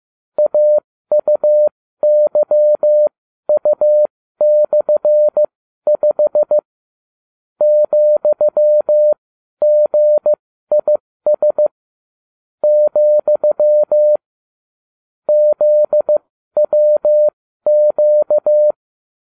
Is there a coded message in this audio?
Yes: Morse code from 0.4 to 18.7 s, heard as a keyed tone, about 15 wpm, spaced as for 12 wpm, 600 Hz. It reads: AUYU/5 ,GIS , ZWQ